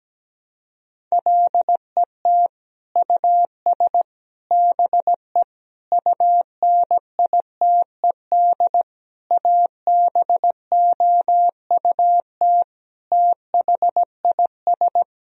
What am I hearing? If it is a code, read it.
Morse code, heard as a keyed tone, about 17 wpm, 700 Hz. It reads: LET US BE UNITED ABOUT THIS
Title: Sentences from Top 1000 Words - Review Part 4 - 17wpm